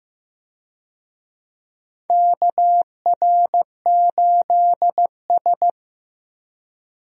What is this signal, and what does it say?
Morse code, heard as a keyed tone, about 15 wpm, 700 Hz. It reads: KR8S
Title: USA Call Signs - 4 Characters - Repeated 3x - 15wpm